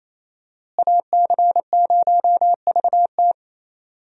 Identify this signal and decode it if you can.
Morse code, heard as a keyed tone, about 28 wpm, 700 Hz. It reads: AC0VT
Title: USA Call Signs - 5 Characters - Repeated 2x - 28wpm